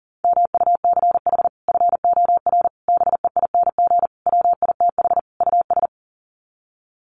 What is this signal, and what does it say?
Morse code, heard as a keyed tone, about 37 wpm, 700 Hz. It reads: MUCH FOR BEING WITH US
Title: Sets of 5 Words 40wpm